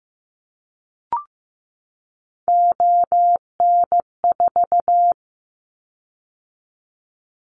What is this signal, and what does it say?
Morse code, heard as a keyed tone, about 15 wpm, 700 Hz. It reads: ON4
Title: Callsign Prefixes for Top 25 DXCC - 15wpm